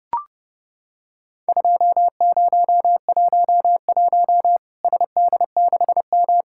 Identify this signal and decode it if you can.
Morse code, heard as a keyed tone, about 32 wpm, 700 Hz. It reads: 2011 SD6M
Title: Worked All Europe DX Contest - QTC Exchange - 30wpm